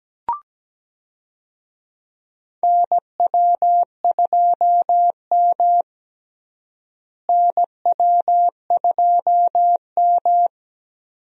Morse code, heard as a keyed tone, about 17 wpm, 700 Hz. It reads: NW2M NW2M
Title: USA Call Signs - 4 Characters - Repeated 3x - 17wpm